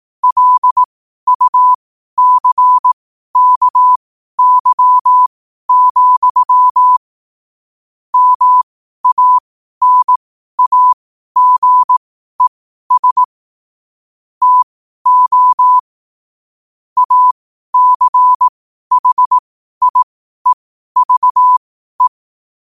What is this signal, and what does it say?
Morse code, heard as a keyed tone, about 18 wpm, spaced as for 12 wpm, 1000 Hz. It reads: LUCKY, MANAGES TO ACHIEVE